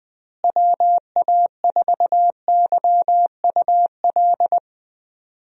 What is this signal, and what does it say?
Morse code, heard as a keyed tone, about 20 wpm, 700 Hz. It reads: WA4YUL
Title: USA Call Signs - 6 Characters - Repeated 3x - 20wpm